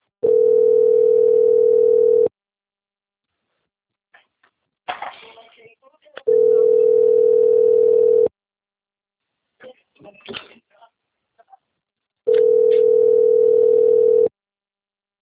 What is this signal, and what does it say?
Ringback tone (North America), 440 + 480 Hz, with speech.
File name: out- - - - - .5327